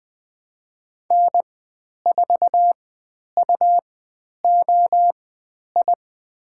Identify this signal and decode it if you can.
Morse code, heard as a keyed tone, about 20 wpm, spaced as for 10 wpm, 700 Hz. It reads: N4UOI
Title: 102 Lesson 17g - Letter u - Call Signs 20wpm